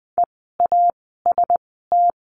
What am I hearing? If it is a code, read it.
Morse code, heard as a keyed tone, about 20 wpm, spaced as for 14 wpm, 700 Hz. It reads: EAST
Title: 068 Lesson 12f - Letter d - Words 20wpm